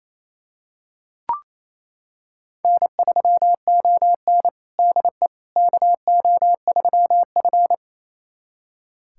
Morse code, heard as a keyed tone, about 27 wpm, 690 Hz. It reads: N3ON DE KO3F
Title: USA Callsign DE Callsign - 4 Characters - 28wpm